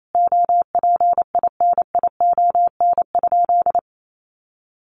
Morse code, heard as a keyed tone, about 28 wpm, 700 Hz. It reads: OPINION?
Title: Sentences from Top 1200 Words 28wpm